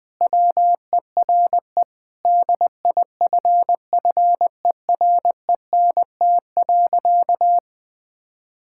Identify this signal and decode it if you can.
Morse code, heard as a keyed tone, about 21 wpm, 700 Hz. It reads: WERE DIFFERENT.
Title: Sentences with 6-Word Phrases - 20wpm